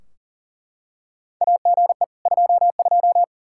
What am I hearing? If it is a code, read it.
Morse code, heard as a keyed tone, about 40 wpm, 700 Hz. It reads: AGE 22
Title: Practice QSO - Exchange 3 - More Info - 40wpm